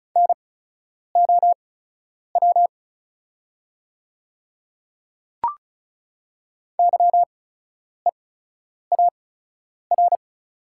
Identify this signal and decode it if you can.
Morse code, heard as a keyed tone, about 36 wpm, spaced as for 10 wpm, 700 Hz. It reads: NOW YEAR
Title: 8x Character Spacing - Top 200 Words - 35wpm